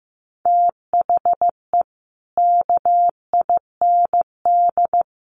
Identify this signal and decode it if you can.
Morse code, heard as a keyed tone, about 15 wpm, 700 Hz. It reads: THE KIND
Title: Sets of 5 Words Encore 15wpm